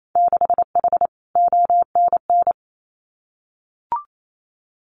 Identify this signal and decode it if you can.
Morse code, heard as a keyed tone, about 28 wpm, 700 Hz. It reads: 6H ONN
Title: Winter Field Day Exchanges Encore 28wpm